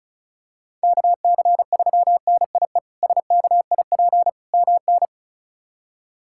Morse code, heard as a keyed tone, about 34 wpm, 700 Hz. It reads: KC3NIE SKIP MN
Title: K1USN SST QSO Practice - 35wpm